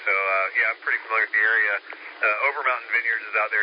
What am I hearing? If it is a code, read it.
Emergency Alert System (SAME) tones, heard as harsh buzzing bursts, with speech. It is heard from 0.5 to 3.2 s.